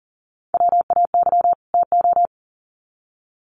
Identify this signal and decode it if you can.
Morse code, heard as a keyed tone, about 40 wpm, 700 Hz. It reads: WAY TO
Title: Sets of 2 Words 40wpm